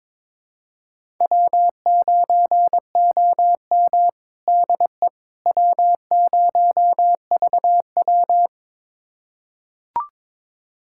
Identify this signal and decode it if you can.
Morse code, heard as a keyed tone, about 22 wpm, 700 Hz. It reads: W9OM DE W0VW
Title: USA Callsign DE Callsign - 4 Characters - 22wpm